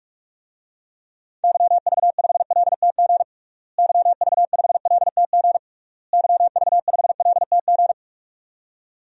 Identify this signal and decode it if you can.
Morse code, heard as a keyed tone, about 46 wpm, 700 Hz. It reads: YV5LTG YV5LTG YV5LTG
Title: DXCC-100 Call Signs - 6 Characters - Repeated 3x - 45wpm